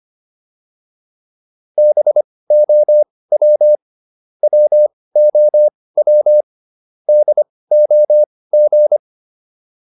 Morse code, heard as a keyed tone, about 25 wpm, spaced as for 18 wpm, 600 Hz. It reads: BOW WOW DOG